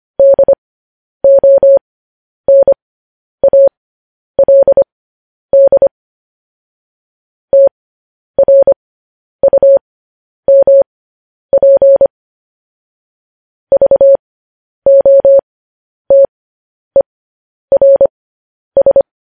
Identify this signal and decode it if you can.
Morse code, heard as a keyed tone, about 25 wpm, spaced as for 10 wpm, 550 Hz. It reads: DONALD TRUMP VOTERS
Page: news headlines in morse code at 10 wpm